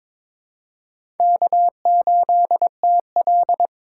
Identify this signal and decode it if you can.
Morse code, heard as a keyed tone, about 21 wpm, 700 Hz. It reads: K8TL
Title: USA Call Signs - 4 Characters - 4x Character Spacing - 22wpm